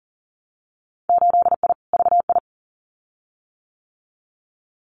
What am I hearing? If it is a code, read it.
Morse code, heard as a keyed tone, about 42 wpm, 700 Hz. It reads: 8I VI
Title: Winter Field Day Exchanges Encore 40wpm